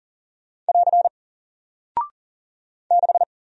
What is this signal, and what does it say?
Morse code, heard as a keyed tone, about 40 wpm, 700 Hz. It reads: <AR> 6
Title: Single Character (Letter:Number:Punctuation:Prosign) 40wpm